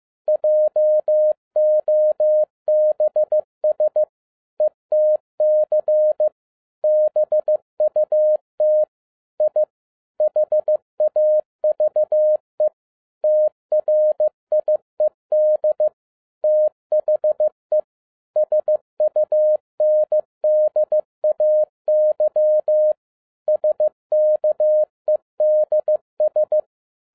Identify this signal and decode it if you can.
Morse code, heard as a keyed tone, about 15 wpm, 610 Hz. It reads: JOBS ETC BUT I HAVE TRIED THE SUNDAY SKEDS